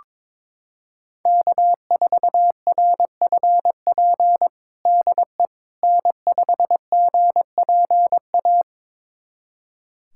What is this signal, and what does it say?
Morse code, heard as a keyed tone, about 22 wpm, 700 Hz. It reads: K4RFP DE N5GPA
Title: USA Callsign DE Callsign - 5 Characters - 22wpm